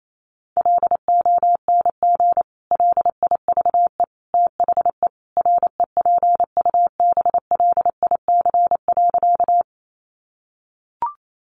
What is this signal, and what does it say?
Morse code, heard as a keyed tone, about 28 wpm, 700 Hz. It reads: LONG LIVE THE REPUBLIC.
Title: Sentences from Top 1500 Words 28wpm